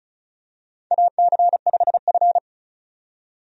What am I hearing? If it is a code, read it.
Morse code, heard as a keyed tone, about 36 wpm, 700 Hz. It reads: AC5F
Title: USA Call Signs - 4 Characters - 35wpm